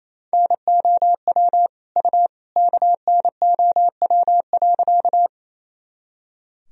Morse code, heard as a keyed tone, about 28 wpm, 700 Hz. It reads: NOW U KNOW.
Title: Sentences from Top 100 Words and Abbreviations 28wpm